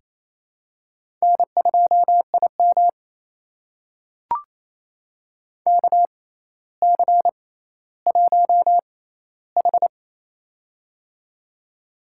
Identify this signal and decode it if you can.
Morse code, heard as a keyed tone, about 28 wpm, spaced as for 14 wpm, 700 Hz. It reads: N2IM KC1H
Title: USA Call Signs - 4 Characters - 6x Character Spacing - 28wpm